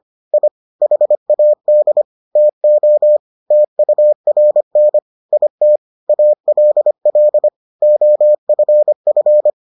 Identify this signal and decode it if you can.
Morse code, heard as a keyed tone, about 25 wpm, 600 Hz. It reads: I HAD TO TURN IT ALL OFF